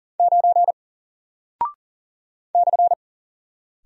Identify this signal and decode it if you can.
Morse code, heard as a keyed tone, about 40 wpm, 700 Hz. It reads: 9 /
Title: Single Character (Letter:Number:Punctuation:Prosign) Alt 40wpm